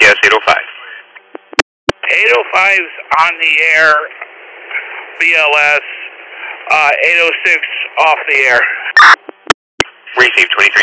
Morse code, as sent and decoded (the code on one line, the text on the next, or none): none